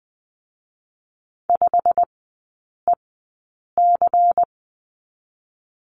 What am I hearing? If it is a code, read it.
Morse code, heard as a keyed tone, about 19 wpm, 700 Hz. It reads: 5 E C
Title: 121 Lesson 20e - Number 3 - Three Characters 20wpm